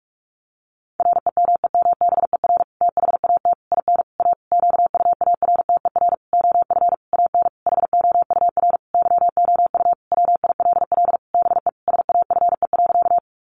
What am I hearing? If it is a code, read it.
Morse code, heard as a keyed tone, about 44 wpm, 700 Hz. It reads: REMEMBER THAT IN A QUARTER OF AN HOUR YOU WILL BE SAFE.
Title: Sentences from Top 1300 Words 45wpm